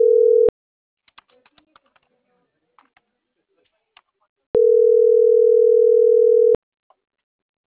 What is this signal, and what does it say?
Ringback tone (North America), 440 + 480 Hz.